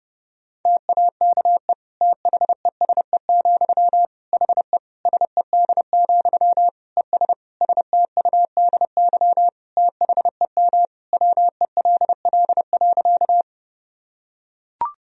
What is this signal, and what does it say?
Morse code, heard as a keyed tone, about 30 wpm, 700 Hz. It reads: TAKE THESE, HE SED, ES STUDY THEM WELL.
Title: Sentences from Top 300 Words and Abbreviations 30wpm